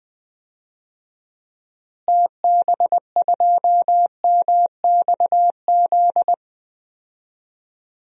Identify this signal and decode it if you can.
Morse code, heard as a keyed tone, about 20 wpm, 700 Hz. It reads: TB2MXZ
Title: DXCC-100 Call Signs - 6 Characters - Repeated 2x - 20wpm